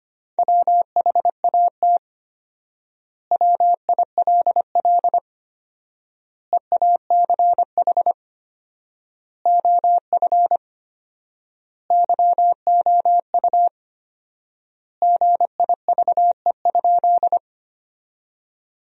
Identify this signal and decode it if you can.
Morse code, heard as a keyed tone, about 25 wpm, 700 Hz. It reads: WHAT WILL EACH OF YOU GIVE?
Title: 4x Word-Spacing - Sentences from Top 200 Words - 25wpm